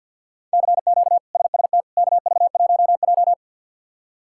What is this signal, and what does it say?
Morse code, heard as a keyed tone, about 49 wpm, 700 Hz. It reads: CQ SST KV1J
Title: K1USN SST QSO Practice - 50wpm